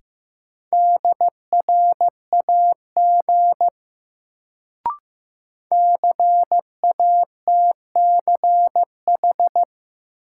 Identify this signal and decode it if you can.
Morse code, heard as a keyed tone, about 15 wpm, 700 Hz. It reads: DRAG CATCH
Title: Sentences from Top 1500 Words - New Word Drill - 15wpm